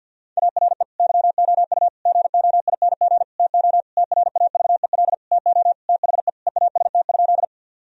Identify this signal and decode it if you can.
Morse code, heard as a keyed tone, about 52 wpm, 700 Hz. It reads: ARE YOU GOING TO TRAVEL TO THE EAST?
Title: Sentences from Top 1000 Words - Review Part 3 - 50wpm